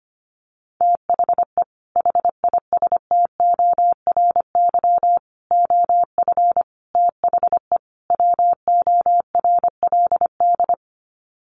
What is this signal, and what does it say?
Morse code, heard as a keyed tone, about 25 wpm, 700 Hz. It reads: THE HISTORY OF THE WORLD
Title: Sets of 5 Words Encore2 25wpm